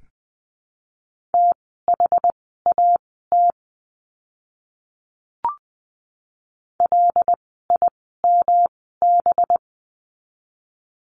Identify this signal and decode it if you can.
Morse code, heard as a keyed tone, about 20 wpm, spaced as for 15 wpm, 700 Hz. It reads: THAT LIMB
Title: 185 Lesson 30f - Letter b - Words 20wpm